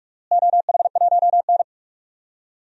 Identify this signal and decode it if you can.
Morse code, heard as a keyed tone, about 46 wpm, 700 Hz. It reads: OH1N
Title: DXCC-100 Call Signs - 4 Characters - Repeated 3x - 45wpm